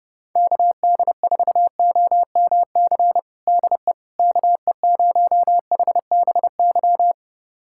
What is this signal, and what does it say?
Morse code, heard as a keyed tone, about 30 wpm, 700 Hz. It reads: KD4OMC DE KE0HBY
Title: USA Callsign DE Callsign - 6 Characters - 30wpm